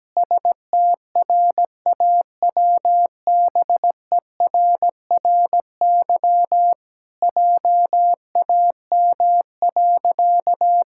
Morse code, heard as a keyed tone, about 17 wpm, 700 Hz. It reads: STRAWBERRY JAM.
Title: Pangram Challenge v2 - 17wpm